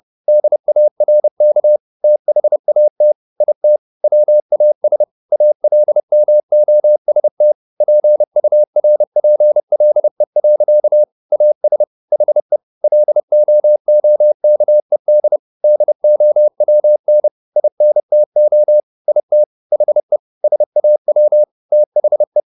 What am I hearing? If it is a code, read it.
Morse code, heard as a keyed tone, about 30 wpm, 600 Hz. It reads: DARK THAT IT WAS ALMOST PURPLE. AS HE LOOKED DOWN INTO IT HE SAW THE
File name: the old man and the sea / the old man and the sea 1 30wpm